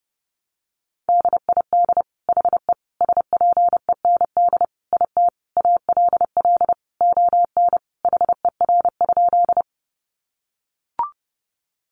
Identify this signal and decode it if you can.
Morse code, heard as a keyed tone, about 30 wpm, 700 Hz. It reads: DID HE SPEND IT ALL ON HER?
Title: Sentences from Top 800 Words 30wpm